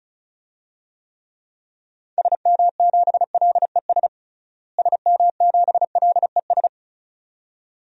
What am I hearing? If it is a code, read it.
Morse code, heard as a keyed tone, about 36 wpm, 700 Hz. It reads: SM7LES SM7LES
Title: DXCC-100 Call Signs - 6 Characters - Repeated 2x - 35wpm